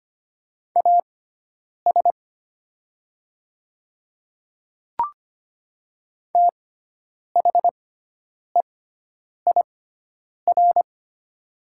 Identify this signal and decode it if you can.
Morse code, heard as a keyed tone, about 25 wpm, spaced as for 9 wpm, 700 Hz. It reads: AS THEIR
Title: 6x Character Spacing - Top 200 Words - 25wpm